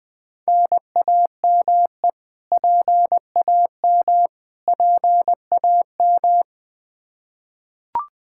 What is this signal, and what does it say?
Morse code, heard as a keyed tone, about 20 wpm, 700 Hz. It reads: NAME PAM PAM
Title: Practice QSO - Exchange 1 - Beginning - 20wpm